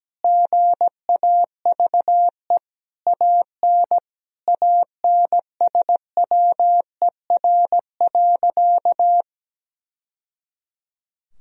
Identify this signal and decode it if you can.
Morse code, heard as a keyed tone, about 17 wpm, 700 Hz. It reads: GAVE AN ANSWER.
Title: Sentences from Top 1000 Words - Review Part 4 - 17wpm